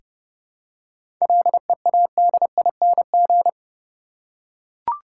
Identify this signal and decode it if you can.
Morse code, heard as a keyed tone, about 30 wpm, 700 Hz. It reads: LEADING
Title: Seven Letter Words - 30wpm